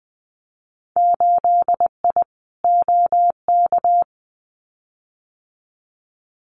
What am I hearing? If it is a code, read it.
Morse code, heard as a keyed tone, about 20 wpm, 700 Hz. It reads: 8I OK
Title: Winter Field Day Exchanges Encore 20wpm